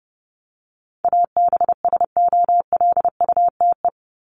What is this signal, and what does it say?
Morse code, heard as a keyed tone, about 30 wpm, 700 Hz. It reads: ABSOLUTE
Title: LY Suffix - Top 100 Words with Base Words 30wpm